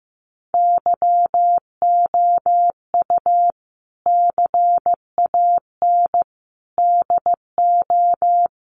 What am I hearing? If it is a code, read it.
Morse code, heard as a keyed tone, about 15 wpm, 700 Hz. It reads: YOU CAN DO
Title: Sentences from Top 100 Words 15wpm